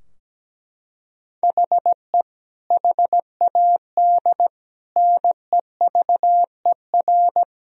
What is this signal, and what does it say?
Morse code, heard as a keyed tone, about 17 wpm, 700 Hz. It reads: HE HAD NEVER
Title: Sentences from Top 1000 Words - Review Part 3 - 17wpm